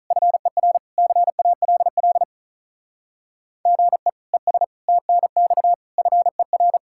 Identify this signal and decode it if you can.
Morse code, heard as a keyed tone, about 43 wpm, 700 Hz. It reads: FER CALL GE ES TNX FER
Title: Speed-Racing Common QSO Phrases Encore - 35wpm